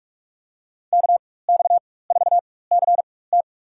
Morse code, heard as a keyed tone, about 45 wpm, 700 Hz. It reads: K X 4 C T
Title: USA Call Signs - 5 Characters - 4x Character Spacing - 45wpm